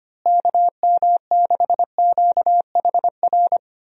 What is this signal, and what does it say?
Morse code, heard as a keyed tone, about 25 wpm, 700 Hz. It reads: KM6QHR
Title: USA Call Signs - 6 Characters - Repeated 2x - 25wpm